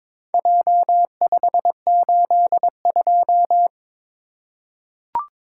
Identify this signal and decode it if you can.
Morse code, heard as a keyed tone, about 23 wpm, 700 Hz. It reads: J582
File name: Four Letter-Number Combinations - 22wpm